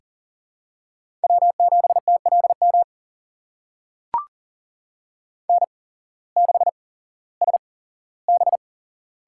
Morse code, heard as a keyed tone, about 40 wpm, spaced as for 19 wpm, 690 Hz. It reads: W7TLM N6SB
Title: USA Call Signs - 8x Character Spacing - 40wpm